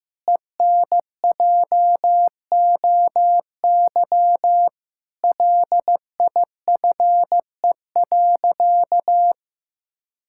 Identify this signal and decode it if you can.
Morse code, heard as a keyed tone, about 15 wpm, 700 Hz. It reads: ENJOY LIFE.
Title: Sentences from Top 1000 Words - Review Part 5 - 15wpm